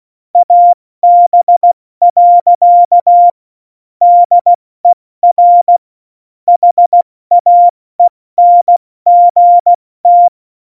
Morse code, heard as a keyed tone, about 16 wpm, spaced as for 14 wpm, 700 Hz. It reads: AB. DER HAENGT
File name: Morsekurs / Witze / Witz 08. Dezember 2020, 70 F80